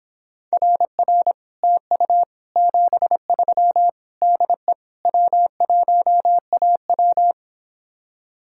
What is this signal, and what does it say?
Morse code, heard as a keyed tone, about 26 wpm, 700 Hz. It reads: RR TU 73 DE W1AW